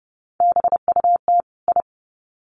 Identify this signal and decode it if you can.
Morse code, heard as a keyed tone, about 30 wpm, 700 Hz. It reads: BUT I
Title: Sets of 2 Words 30wpm